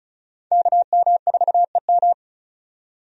Morse code, heard as a keyed tone, about 36 wpm, 700 Hz. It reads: KM4EM
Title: USA Call Signs - 5 Characters - 35wpm